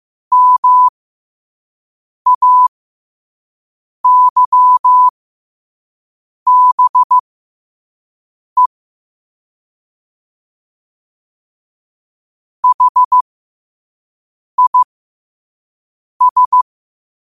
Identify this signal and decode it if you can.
Morse code, heard as a keyed tone, about 15 wpm, spaced as for 5 wpm, 1000 Hz. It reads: MAYBE HIS
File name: qotdfiles / 5wpm